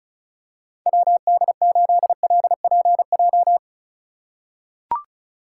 Morse code, heard as a keyed tone, about 34 wpm, 700 Hz. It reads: WD8LPJ
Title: USA Call Signs - 6x Character Spacing - 35wpm